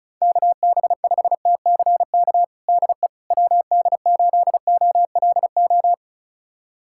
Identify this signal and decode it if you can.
Morse code, heard as a keyed tone, about 36 wpm, 690 Hz. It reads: KB5TCK DE WD8OLO
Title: USA Callsign DE Callsign - 6 Characters - 35wpm